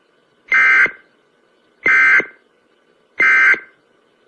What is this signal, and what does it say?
Emergency Alert System (SAME) tones, heard as harsh buzzing bursts. It is heard from 0.5 to 3.7 s.